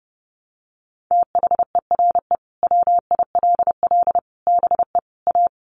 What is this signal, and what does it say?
Morse code, heard as a keyed tone, about 28 wpm, 700 Hz. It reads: THERE WILL BE A
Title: Sets of 4 Words 30wpm